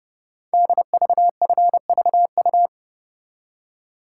Morse code, heard as a keyed tone, about 29 wpm, 700 Hz. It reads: DVFVU
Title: Code Groups - 5 Letters - 30wpm